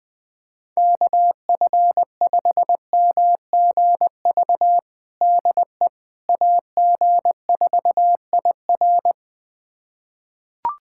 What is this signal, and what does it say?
Morse code, heard as a keyed tone, about 20 wpm, 700 Hz. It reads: KF5MGV DE AG4IR
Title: USA Callsign DE Callsign 20wpm